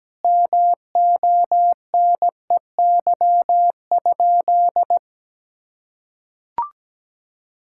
Morse code, heard as a keyed tone, about 17 wpm, 700 Hz. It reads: MONEY?
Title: Sentences from Top 1600 Words 17wpm